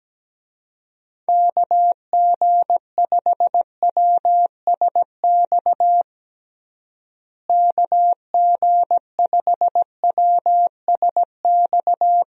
Morse code, heard as a keyed tone, about 17 wpm, 700 Hz. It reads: KG5WSX KG5WSX
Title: USA Call Signs - 6 Characters - Repeated 3x - 17wpm